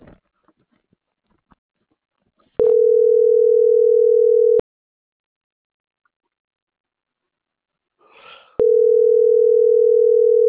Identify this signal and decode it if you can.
Ringback tone (North America), 440 + 480 Hz.